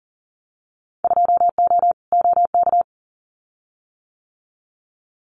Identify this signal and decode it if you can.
Morse code, heard as a keyed tone, about 40 wpm, 700 Hz. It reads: 2O OK
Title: Winter Field Day Exchanges Encore 40wpm